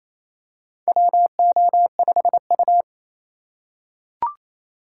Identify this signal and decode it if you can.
Morse code, heard as a keyed tone, about 28 wpm, 700 Hz. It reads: WO5U
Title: USA Call Signs - 4 Characters - 2x Character Spacing - 28wpm